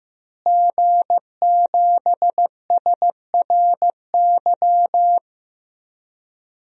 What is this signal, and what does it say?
Morse code, heard as a keyed tone, about 15 wpm, 700 Hz. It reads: G7SRY
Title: DXCC-100 Call Signs - 5 Characters - Repeated 2x - 15wpm